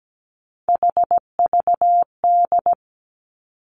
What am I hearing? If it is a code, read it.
Morse code, heard as a keyed tone, about 17 wpm, 700 Hz. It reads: HVD
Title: Three Challenging Letter-Number Combinations 17wpm